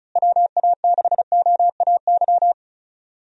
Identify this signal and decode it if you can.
Morse code, heard as a keyed tone, about 34 wpm, 700 Hz. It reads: WA6OAY
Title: USA Call Signs - 6 Characters - Repeated 2x - 35wpm